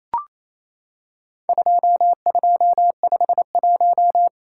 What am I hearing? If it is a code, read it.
Morse code, heard as a keyed tone, about 28 wpm, 700 Hz. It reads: 2251
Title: Worked All Europe DX Contest - QTC Exchange - 28wpm